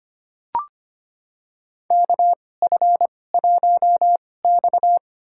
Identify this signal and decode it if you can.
Morse code, heard as a keyed tone, about 25 wpm, spaced as for 18 wpm, 700 Hz. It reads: KF1X
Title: USA Call Signs - 4 Characters - 2x Character Spacing - 25wpm